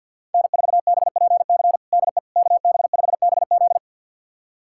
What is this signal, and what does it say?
Morse code, heard as a keyed tone, about 45 wpm, 700 Hz. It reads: N4BPC DE KB5BZ
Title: USA Callsign DE Callsign - 5 Characters - 50wpm